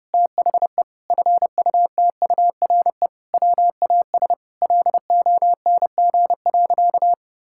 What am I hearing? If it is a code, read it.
Morse code, heard as a keyed tone, about 30 wpm, 700 Hz. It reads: THE FUTURE WAS LONG.
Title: Sentences from Top 1000 Words - Review Part 6 - 30wpm